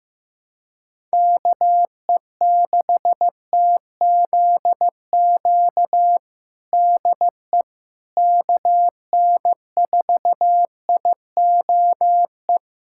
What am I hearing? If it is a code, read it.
Morse code, heard as a keyed tone, about 15 wpm, 700 Hz. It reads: KE6TZQ DE KN4IOE
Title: USA Callsign DE Callsign 15wpm